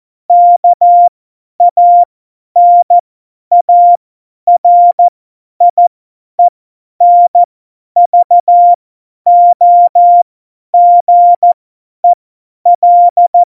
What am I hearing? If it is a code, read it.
Morse code, heard as a keyed tone, about 14 wpm, spaced as for 10 wpm, 700 Hz. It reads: KANARIENVOGEL
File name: Morsekurs / Witze / Witz 04. September 2018, 50 F70